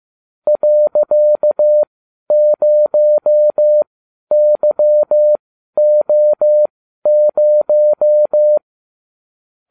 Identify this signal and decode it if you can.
Morse code, heard as a keyed tone, about 15 wpm, spaced as for 11 wpm, 600 Hz. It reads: .0YO0